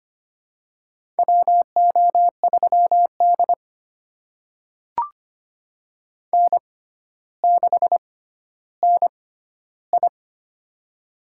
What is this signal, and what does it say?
Morse code, heard as a keyed tone, about 25 wpm, spaced as for 13 wpm, 700 Hz. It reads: WO3D N6NI
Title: USA Call Signs - 6x Character Spacing - 25wpm